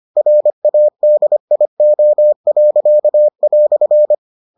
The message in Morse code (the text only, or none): RADIO."